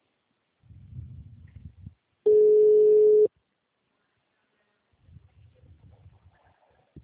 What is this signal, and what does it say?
Ringback tone (Europe, 425 Hz).
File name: out- -2009- - - .58646